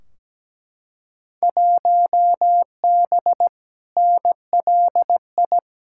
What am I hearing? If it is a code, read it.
Morse code, heard as a keyed tone, about 17 wpm, 700 Hz. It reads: 1B NLI
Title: Field Day Exchanges Encore - 17wpm